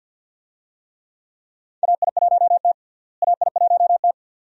Morse code, heard as a keyed tone, about 50 wpm, 700 Hz. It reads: AI1T AI1T
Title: USA Call Signs - 4 Characters - Repeated 2x - 50wpm